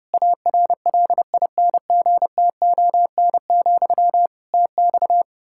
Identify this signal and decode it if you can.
Morse code, heard as a keyed tone, about 30 wpm, 700 Hz. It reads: ARLINGTON, TX
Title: Practice QSO - Exchange 1 - Beginning - 30wpm